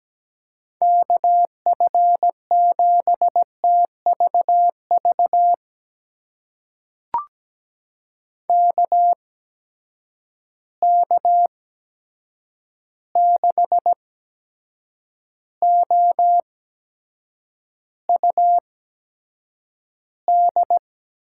Morse code, heard as a keyed tone, about 17 wpm, spaced as for 7 wpm, 700 Hz. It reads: KF7TVV KK6OUD
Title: USA Call Signs - 6 Characters - 8x Character Spacing - 17wpm